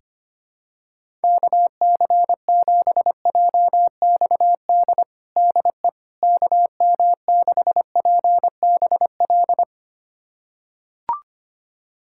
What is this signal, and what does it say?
Morse code, heard as a keyed tone, about 25 wpm, 700 Hz. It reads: KC7JXD DE KM6PBL
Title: USA Callsign DE Callsign - 6 Characters - 25wpm